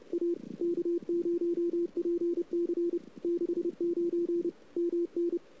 Morse code, heard as a keyed tone, about 30 wpm, 380 Hz. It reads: A K0PC 69 MN
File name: CW / 2008-ss-cw-k5tr-op-n5rz -00:00